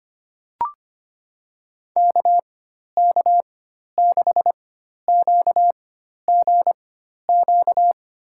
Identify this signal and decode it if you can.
Morse code, heard as a keyed tone, about 25 wpm, spaced as for 12 wpm, 700 Hz. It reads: KK6QGQ
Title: USA Call Signs - 6 Characters - 4x Character Spacing - 25wpm